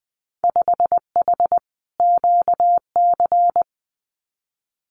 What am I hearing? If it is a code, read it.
Morse code, heard as a keyed tone, about 20 wpm, 700 Hz. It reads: 5H QC
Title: Winter Field Day Exchanges Encore 20wpm